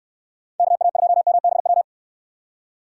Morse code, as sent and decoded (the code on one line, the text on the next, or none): -.- .- ..--- -- -... .--
KA2MBW